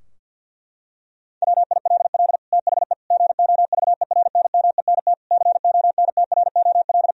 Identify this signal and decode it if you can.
Morse code, heard as a keyed tone, about 48 wpm, 700 Hz. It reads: WILL THE GOVERNMENT CONTROL
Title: Sentences from Top 1000 Words - Review Part 3 - 50wpm